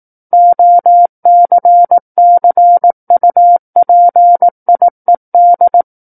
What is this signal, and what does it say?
Morse code, heard as a keyed tone, about 18 wpm, 700 Hz. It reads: OCCUPIED
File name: Book One / 09-Chapter 9 - 18 WPM